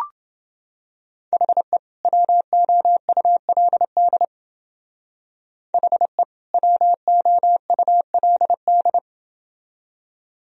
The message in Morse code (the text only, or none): HE WOULD HE WOULD